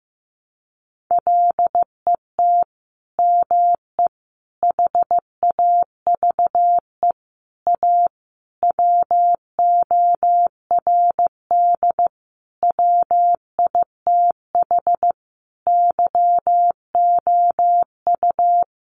Morse code, heard as a keyed tone, about 15 wpm, 700 Hz. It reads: LET ME HAVE A WORD WITH YOU
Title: Sentences from Top 900 Words 15wpm